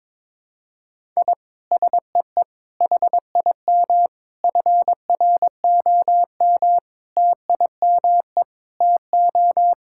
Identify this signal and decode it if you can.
Morse code, heard as a keyed tone, about 22 wpm, 700 Hz. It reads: I SEE HIM FROM TIME TO
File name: practice / 22wpm / Sentences with 4-Word Phrases - 22wpm